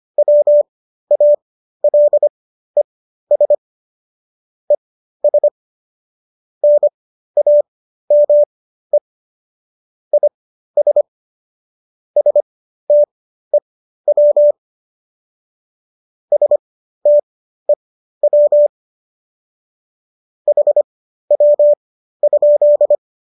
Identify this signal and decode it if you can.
Morse code, heard as a keyed tone, about 25 wpm, spaced as for 13 wpm, 590 Hz. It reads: WALES ES NAME IS STEW STEW HW?